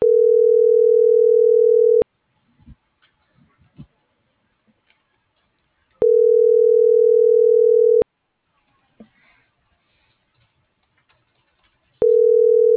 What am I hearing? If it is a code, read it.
Ringback tone (North America), 440 + 480 Hz.